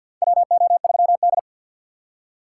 Morse code, heard as a keyed tone, about 50 wpm, 700 Hz. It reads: WO3D
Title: USA Call Signs - 2x Character Spacing - 50wpm